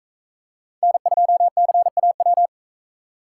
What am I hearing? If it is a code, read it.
Morse code, heard as a keyed tone, about 43 wpm, 700 Hz. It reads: N2CAW